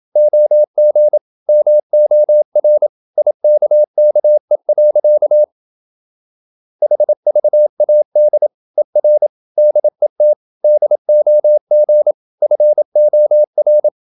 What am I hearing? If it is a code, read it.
Morse code, heard as a keyed tone, about 27 wpm, 600 Hz. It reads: OG MOR IKKE. HVAD ER DET DOG FOR
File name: Ferie QRQ 27wpm